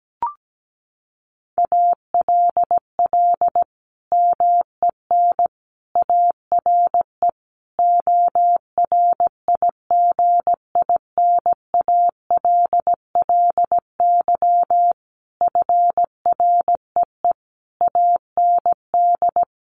Morse code, heard as a keyed tone, about 17 wpm, 700 Hz. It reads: ALL MEN ARE ORIGINALLY FREE AND
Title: Sentences from Top 1500 Words 17wpm